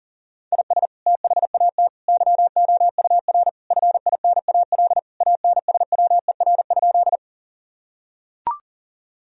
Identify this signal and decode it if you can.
Morse code, heard as a keyed tone, about 40 wpm, 700 Hz. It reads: IS THAT YOUR FINAL ANSWER?
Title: Sentences from Top 1000 Words - Review Part 2 - 40wpm